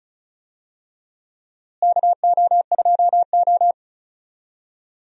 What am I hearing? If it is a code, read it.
Morse code, heard as a keyed tone, about 36 wpm, 700 Hz. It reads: KO2O
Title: USA Call Signs - 4 Characters - Repeated 3x - 35wpm